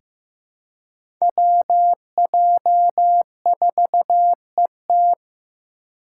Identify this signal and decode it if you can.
Morse code, heard as a keyed tone, about 16 wpm, 700 Hz. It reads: WJ4ET
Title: USA Call Signs - 5 Characters - 4x Character Spacing - 15wpm